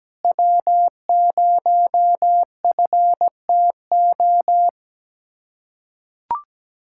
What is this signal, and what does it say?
Morse code, heard as a keyed tone, about 17 wpm, 700 Hz. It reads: W0FTO
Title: USA Callsign DE Callsign - 5 Characters - 17wpm